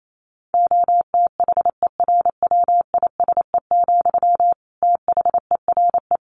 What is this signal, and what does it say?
Morse code, heard as a keyed tone, about 28 wpm, 700 Hz. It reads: OTHERWISE, THERE
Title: Sentences from Top 1400 Words 28wpm